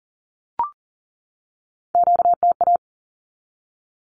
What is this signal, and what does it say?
Morse code, heard as a keyed tone, about 39 wpm, 700 Hz. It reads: QTA
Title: Q-Codes 40wpm